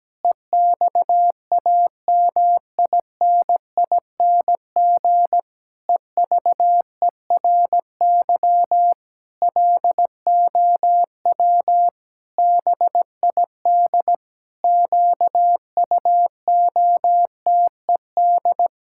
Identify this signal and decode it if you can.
Morse code, heard as a keyed tone, about 17 wpm, 700 Hz. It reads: EXAMINING EVERY LOW BID QUOTED
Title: Pangram Challenge v2 - 17wpm